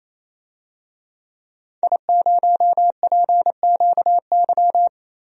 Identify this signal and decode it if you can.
Morse code, heard as a keyed tone, about 28 wpm, 700 Hz. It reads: I0PQY
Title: DXCC-100 Call Signs - 5 Characters - 28wpm